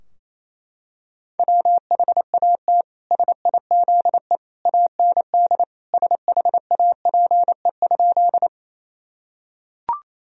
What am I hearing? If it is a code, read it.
Morse code, heard as a keyed tone, about 28 wpm, 700 Hz. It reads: WHAT SIZE AND SHAPE?
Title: Sentences from Top 1000 Words - Review Part 3 - 28wpm